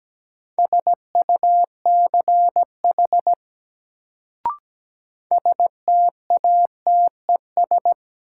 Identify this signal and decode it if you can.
Morse code, heard as a keyed tone, about 17 wpm, 700 Hz. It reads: SUCH STATES
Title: Sets of 2 Words - New Word Drill - 17wpm